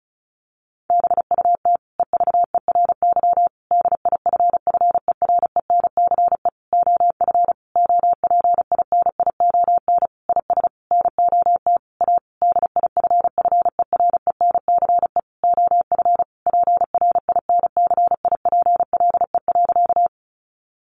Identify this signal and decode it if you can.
Morse code, heard as a keyed tone, about 34 wpm, 700 Hz. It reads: BUT EVERY DIFFERENCE OF OPINION IS NOT A DIFFERENCE OF PRINCIPLE.
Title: Sentences from Top 1400 Words 35wpm